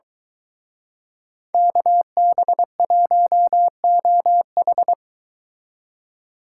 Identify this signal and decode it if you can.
Morse code, heard as a keyed tone, about 23 wpm, 700 Hz. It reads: KB1OH